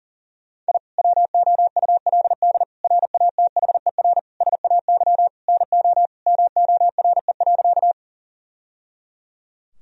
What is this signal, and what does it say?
Morse code, heard as a keyed tone, about 37 wpm, 700 Hz. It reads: I WOULD RATHER SAY NO MORE.
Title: Sentences from Top 1000 Words - Review Part 4 - 40wpm